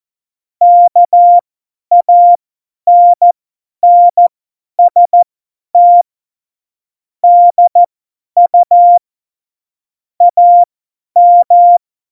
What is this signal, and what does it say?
Morse code, heard as a keyed tone, about 14 wpm, spaced as for 10 wpm, 700 Hz. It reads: KANNST DU AM
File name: Morsekurs / Witze / Witz 11 August 2020, 50 F70